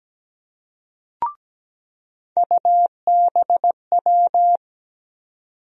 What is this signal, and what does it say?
Morse code, heard as a keyed tone, about 17 wpm, 700 Hz. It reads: UBW
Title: USA Call Sign Postfixes - 3 Letters 17wpm